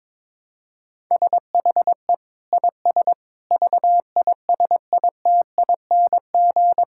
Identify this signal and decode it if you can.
Morse code, heard as a keyed tone, about 21 wpm, 700 Hz. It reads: SHE IS VISITING
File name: practice / 22wpm / Sentences with 4-Word Phrases - 22wpm